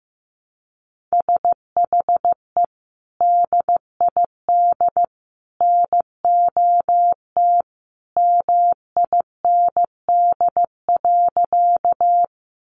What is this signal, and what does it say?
Morse code, heard as a keyed tone, about 15 wpm, 700 Hz. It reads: SHE DID NOT MIND.
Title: Sentences from Top 600 Words 15wpm